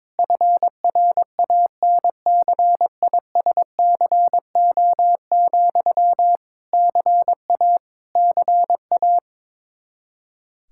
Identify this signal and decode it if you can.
Morse code, heard as a keyed tone, about 22 wpm, 700 Hz. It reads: FRANCISCO, CA CA
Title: Practice QSO - Exchange 1 - Beginning - 22wpm